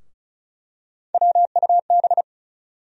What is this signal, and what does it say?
Morse code, heard as a keyed tone, about 36 wpm, 700 Hz. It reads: WUB
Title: USA Call Sign Postfixes - 3 Letters 35wpm